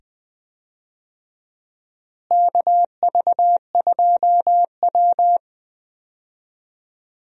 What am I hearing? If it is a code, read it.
Morse code, heard as a keyed tone, about 20 wpm, 700 Hz. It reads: KV2W
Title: USA Call Signs - 4 Characters - Repeated 2x - 20wpm